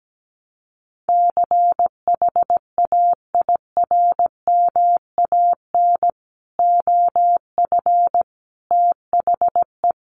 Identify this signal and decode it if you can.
Morse code, heard as a keyed tone, about 17 wpm, 700 Hz. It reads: CHAIRMAN OF THE
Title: Sets of 3 Words 17wpm